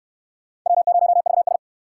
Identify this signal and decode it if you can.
Morse code, heard as a keyed tone, about 59 wpm, 700 Hz. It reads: W0VS